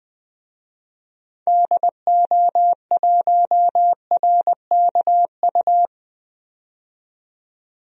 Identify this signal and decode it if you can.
Morse code, heard as a keyed tone, about 20 wpm, 700 Hz. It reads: DO1RKU
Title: DXCC-100 Call Signs - 6 Characters - 20wpm